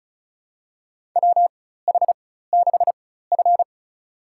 Morse code, heard as a keyed tone, about 35 wpm, spaced as for 16 wpm, 700 Hz. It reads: WH6F
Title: USA Call Signs - 4 Characters - 4x Character Spacing - 35wpm